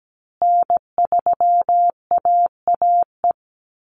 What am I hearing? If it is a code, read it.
Morse code, heard as a keyed tone, about 17 wpm, 700 Hz. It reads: N3AAE
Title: USA Call Signs Repeated 3x 17wpm